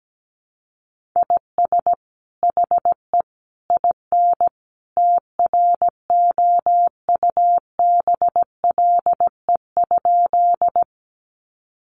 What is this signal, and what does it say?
Morse code, heard as a keyed tone, about 17 wpm, 700 Hz. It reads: IS HE IN TROUBLE?
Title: Sentences from Top 1200 Words 17wpm